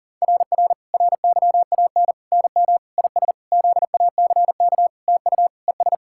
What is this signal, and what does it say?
Morse code, heard as a keyed tone, about 40 wpm, 700 Hz. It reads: RR RYAN NM IS ZACK TU ES
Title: SOTA QSO Practice 40wpm